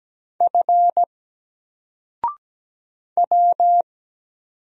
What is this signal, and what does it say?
Morse code, heard as a keyed tone, about 17 wpm, 700 Hz. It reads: F W
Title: Single Character (Letter:Number:Punctuation:Prosign) 17wpm